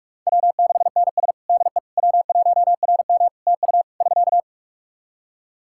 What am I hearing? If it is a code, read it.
Morse code, heard as a keyed tone, about 41 wpm, 700 Hz. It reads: W6NS DE W1RM TU <SK>
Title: Practice QSO - Exchange 4 - Ending - 45wpm